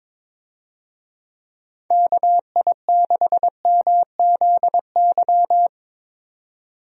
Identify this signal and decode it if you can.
Morse code, heard as a keyed tone, about 22 wpm, 700 Hz. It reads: KI6MZY
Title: USA Call Signs - 6 Characters - 22wpm